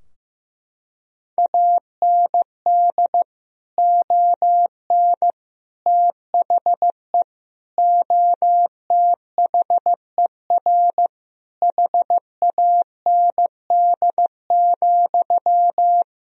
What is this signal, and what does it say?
Morse code, heard as a keyed tone, about 15 wpm, 700 Hz. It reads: AND ON THE OTHER HAND,
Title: Sentences from Top 1000 Words - Review Part 3 - 15wpm